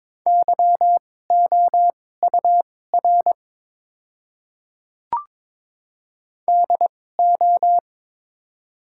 Morse code, heard as a keyed tone, about 22 wpm, spaced as for 16 wpm, 700 Hz. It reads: YOUR DO
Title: 2x Character Spacing - Top 100 Words - 22wpm